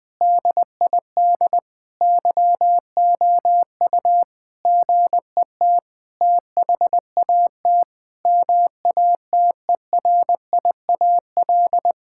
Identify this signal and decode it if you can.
Morse code, heard as a keyed tone, about 20 wpm, 700 Hz. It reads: DID YOU GET THAT MATERIAL
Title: Sentences from Top 1000 Words - Review Part 5 - 20wpm